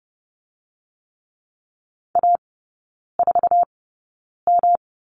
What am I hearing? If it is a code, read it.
Morse code, heard as a keyed tone, about 30 wpm, spaced as for 9 wpm, 700 Hz. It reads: A4M
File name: Lesson 18e - Letter m - Three Characters 30wpm